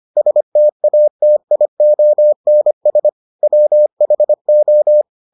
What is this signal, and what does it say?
Morse code, heard as a keyed tone, about 25 wpm, 600 Hz. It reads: STATIONS WHO